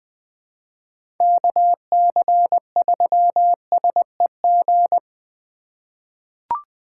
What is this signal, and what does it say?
Morse code, heard as a keyed tone, about 20 wpm, 700 Hz. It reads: KC3SEG
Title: 2023 POTA Activator Call Signs - 20wpm